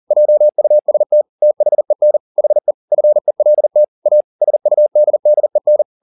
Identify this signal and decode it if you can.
Morse code, heard as a keyed tone, about 40 wpm, 600 Hz. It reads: JUST THEN HE FELT A SUDDEN
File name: the old man and the sea 2 40wpm